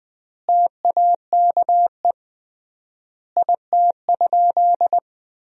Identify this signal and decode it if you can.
Morse code, heard as a keyed tone, about 20 wpm, 700 Hz. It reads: TAKE IT?
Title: 3x Word-Spacing - Sentences from Top 300 Words - 20wpm